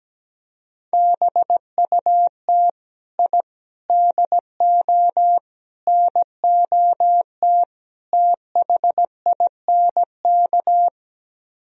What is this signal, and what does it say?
Morse code, heard as a keyed tone, about 17 wpm, 700 Hz. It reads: BUT I DO NOT THINK
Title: Speed-Racing - Sets of 5 Words 17wpm